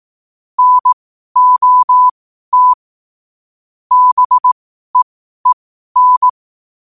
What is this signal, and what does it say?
Morse code, heard as a keyed tone, about 18 wpm, spaced as for 12 wpm, 1000 Hz. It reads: NOT BEEN
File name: qotdfiles / 12wpm